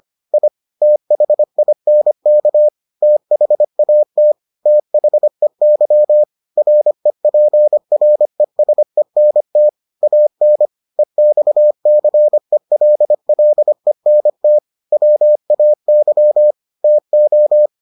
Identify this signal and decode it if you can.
Morse code, heard as a keyed tone, about 25 wpm, 600 Hz. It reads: I THINK THAT THEY REPRESENT AN EXCELLENT WAY TO